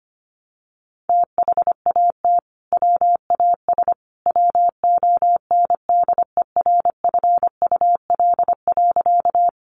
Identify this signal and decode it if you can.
Morse code, heard as a keyed tone, about 25 wpm, 700 Hz. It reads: THAT WAS WONDERFUL.
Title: Sentences from Top 1100 Words 25wpm